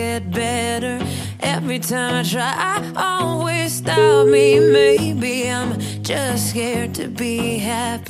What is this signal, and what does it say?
Ringback tone (Europe, 425 Hz), with noise.